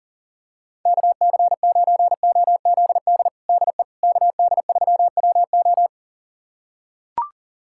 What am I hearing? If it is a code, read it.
Morse code, heard as a keyed tone, about 40 wpm, 700 Hz. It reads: KC9OZD DE KD3WO
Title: USA Callsign DE Callsign 40wpm